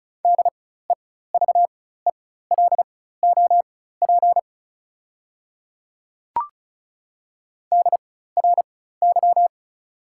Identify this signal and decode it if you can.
Morse code, heard as a keyed tone, about 36 wpm, spaced as for 16 wpm, 690 Hz. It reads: DEVELOP DRY
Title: 4x Character Spacing - Top 500 Words - 35wpm